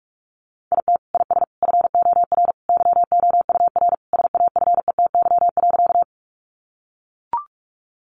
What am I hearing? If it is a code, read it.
Morse code, heard as a keyed tone, about 44 wpm, 700 Hz. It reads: IT IS FOR YOUR SAFETY.
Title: Sentences from Top 700 Words 45wpm